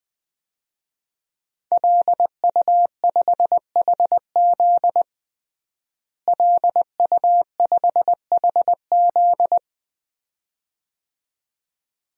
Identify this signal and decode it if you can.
Morse code, heard as a keyed tone, about 20 wpm, 700 Hz. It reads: LU5HZ LU5HZ
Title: DXCC-100 Call Signs - 5 Characters - Repeated 2x - 20wpm